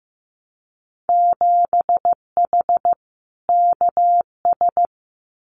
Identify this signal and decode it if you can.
Morse code, heard as a keyed tone, about 15 wpm, 700 Hz. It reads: 7H KS
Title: Winter Field Day Exchanges Encore 15wpm